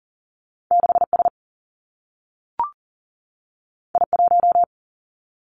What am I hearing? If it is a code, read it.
Morse code, heard as a keyed tone, about 40 wpm, 700 Hz. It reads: 6 S I 1
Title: Two Challenging Letter-Number Combinations 40wpm